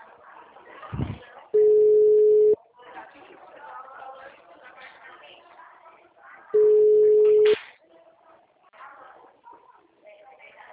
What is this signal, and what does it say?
Ringback tone (Europe, 425 Hz), with speech.
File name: out- -2003- - - .5650